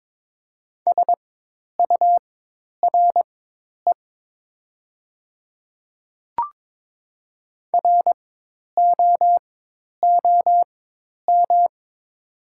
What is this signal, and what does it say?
Morse code, heard as a keyed tone, about 22 wpm, spaced as for 10 wpm, 700 Hz. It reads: SURE ROOM
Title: 4x Character Spacing - Top 300 Words - 22wpm